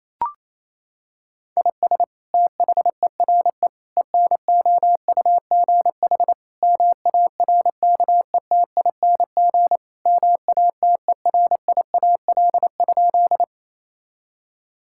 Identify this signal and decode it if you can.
Morse code, heard as a keyed tone, about 28 wpm, 700 Hz. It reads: IS THERE ENOUGH MARKETING MATERIAL?
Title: Sentences from Top 1000 Words - Review Part 5 - 28wpm